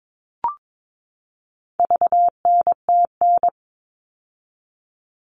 Morse code, heard as a keyed tone, about 22 wpm, 700 Hz. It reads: VNTN
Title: All Cut Numbers - 4-Digit 22wpm